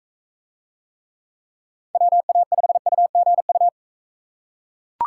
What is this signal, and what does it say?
Morse code, heard as a keyed tone, about 41 wpm, 700 Hz. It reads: WA5UGU